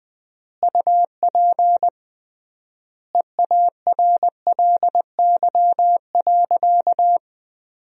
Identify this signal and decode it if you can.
Morse code, heard as a keyed tone, about 20 wpm, 700 Hz. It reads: UP EARLY.
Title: 3x Word-Spacing - Sentences from Top 400 Words - 20wpm